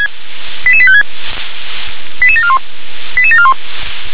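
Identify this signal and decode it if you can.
Five-tone selective call (ZVEI), heard as quick runs of five tones, tones of about 70 ms. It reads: 88856 88741 88741
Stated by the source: Five-tone ZVEI selective call received on SDR.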